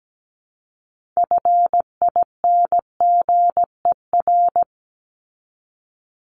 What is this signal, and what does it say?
Morse code, heard as a keyed tone, about 17 wpm, 700 Hz. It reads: FINGER
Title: Top 1501-1750 Words 17wpm